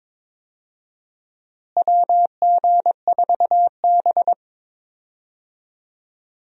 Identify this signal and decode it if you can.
Morse code, heard as a keyed tone, about 22 wpm, 700 Hz. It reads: WG4B
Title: USA Call Signs - 4 Characters - Repeated 2x - 22wpm